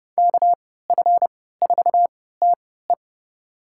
Morse code, heard as a keyed tone, about 30 wpm, spaced as for 17 wpm, 700 Hz. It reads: KF4TE
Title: USA Call Signs - 5 Characters - 3x Character Spacing - 30wpm